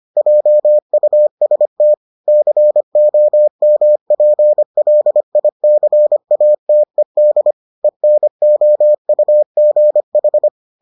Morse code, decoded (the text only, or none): JUST COMPLICATED ENOUGH